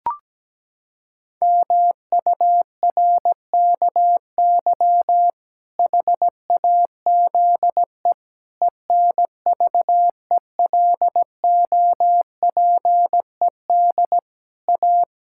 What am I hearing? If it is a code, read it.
Morse code, heard as a keyed tone, about 17 wpm, 700 Hz. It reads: MURKY HAZE ENVELOPED A
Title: Pangram Challenge v2 - 17wpm